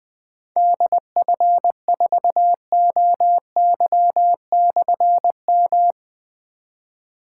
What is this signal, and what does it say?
Morse code, heard as a keyed tone, about 20 wpm, 700 Hz. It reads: DF4OY/M
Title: Difficult Call Signs 20wpm